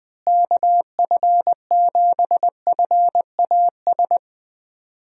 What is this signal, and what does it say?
Morse code, heard as a keyed tone, about 20 wpm, 700 Hz. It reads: KF7FAS
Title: USA Call Signs - 6 Characters - Repeated 2x - 20wpm